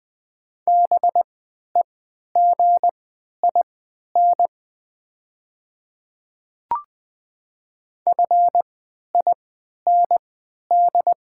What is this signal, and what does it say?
Morse code, heard as a keyed tone, about 20 wpm, spaced as for 11 wpm, 690 Hz. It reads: BEGIN FIND
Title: 3x Character Spacing - Top 300 Words - 20wpm